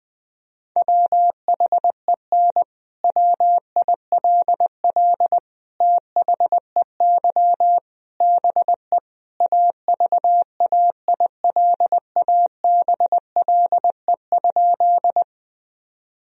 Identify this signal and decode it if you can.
Morse code, heard as a keyed tone, about 20 wpm, 700 Hz. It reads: WHEN WILL THEY BE AVAILABLE?
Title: Sentences from Top 1000 Words - Review Part 3 - 20wpm